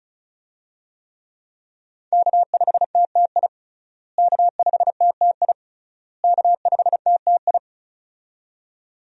Morse code, heard as a keyed tone, about 35 wpm, 700 Hz. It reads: K5TTI K5TTI K5TTI
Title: USA Call Signs - 5 Characters - Repeated 3x - 35wpm